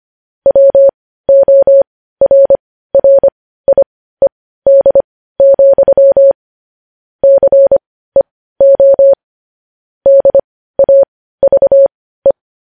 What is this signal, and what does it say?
Morse code, heard as a keyed tone, about 25 wpm, spaced as for 15 wpm, 550 Hz. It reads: WORRIED, CEO DAVE
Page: news-headlines-in-morse-code-at-15-wpm